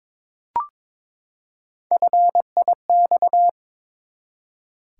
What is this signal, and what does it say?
Morse code, heard as a keyed tone, about 22 wpm, 700 Hz. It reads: FIX
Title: Three Letter Words - 22wpm